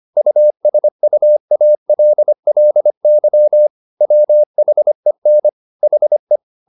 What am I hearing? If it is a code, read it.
Morse code, heard as a keyed tone, about 25 wpm, 600 Hz. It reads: USUALLY WHEN HE
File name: the old man and the sea part 1 25wpm